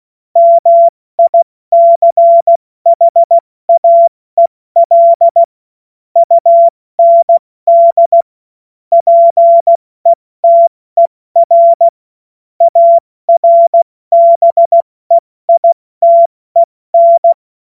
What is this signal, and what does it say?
Morse code, heard as a keyed tone, about 16 wpm, spaced as for 14 wpm, 680 Hz. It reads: MICHAEL UND PETER ARBEITEN
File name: Morsekurs / Witze / Witz 30 Januar 2024 70 F80